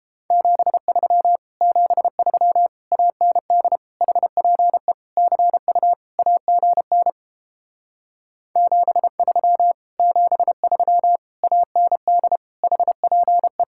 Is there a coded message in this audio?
Yes: Morse code from 0.3 to 13.6 s, heard as a keyed tone, about 29 wpm, 700 Hz. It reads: 73 73 AND HPE CU AGN 73 73 AND HPE